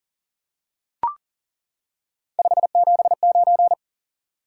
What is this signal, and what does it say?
Morse code, heard as a keyed tone, about 40 wpm, 700 Hz. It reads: 579
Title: Top 180 QSO Elements 40wpm